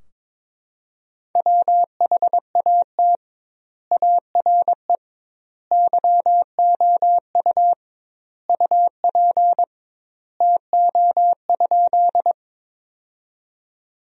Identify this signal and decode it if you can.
Morse code, heard as a keyed tone, about 22 wpm, 700 Hz. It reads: WHAT ARE YOU UP TO?
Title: 2x Word-Spacing - Sentences from Top 100 Words - 22wpm